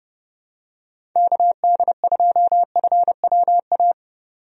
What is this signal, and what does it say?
Morse code, heard as a keyed tone, about 30 wpm, 700 Hz. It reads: KD2FWA
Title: USA Call Signs - 6 Characters - 30wpm